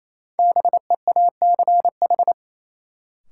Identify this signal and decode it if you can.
Morse code, heard as a keyed tone, about 28 wpm, 700 Hz. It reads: BEACH
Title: Sentences from Top 1500 Words - New Word Drill - 28wpm